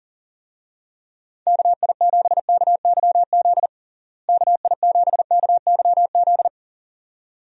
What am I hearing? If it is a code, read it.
Morse code, heard as a keyed tone, about 40 wpm, 700 Hz. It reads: KI7KYZ KI7KYZ
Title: USA Call Signs - 6 Characters - Repeated 2x - 40wpm